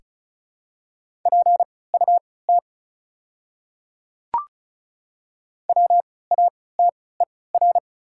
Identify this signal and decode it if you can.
Morse code, heard as a keyed tone, about 36 wpm, spaced as for 20 wpm, 700 Hz. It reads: PUT WATER
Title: 3x Character Spacing - Top 200 Words - 35wpm